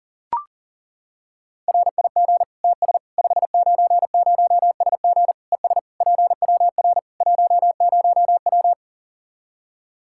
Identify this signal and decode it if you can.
Morse code, heard as a keyed tone, about 40 wpm, 700 Hz. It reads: RIG TS 590SG ES PWR 10W
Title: Practice QSO - Exchange 2 - Info - 40wpm